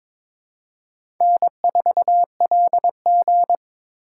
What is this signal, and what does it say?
Morse code, heard as a keyed tone, about 22 wpm, 700 Hz. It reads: N4LG
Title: USA Call Signs - 2x Character Spacing - 22wpm